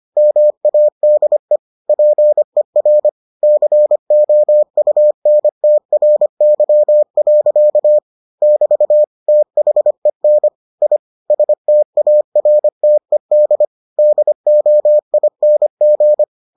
Morse code, heard as a keyed tone, about 25 wpm, 600 Hz. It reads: MADE PER COUNTRY. <BT> THEN I STARTED DOING